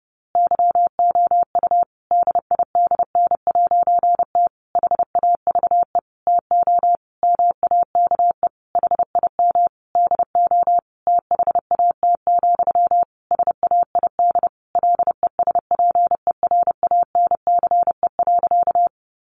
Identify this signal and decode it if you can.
Morse code, heard as a keyed tone, about 30 wpm, 700 Hz. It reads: YOU DIDN'T HAVE TO MAKE HIM DO THAT, SAID LESPERANCE.
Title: A Sound Of Thunder 30wpm